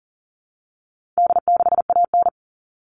Morse code, heard as a keyed tone, about 40 wpm, 700 Hz. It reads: D6AN
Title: All Cut Numbers - 4-Digit 40wpm